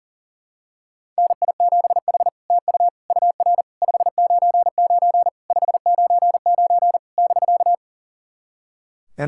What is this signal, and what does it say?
Morse code, heard as a keyed tone, about 37 wpm, 700 Hz. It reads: NI7H TU UR 599 599 <BK>
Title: SOTA QSO Practice 40wpm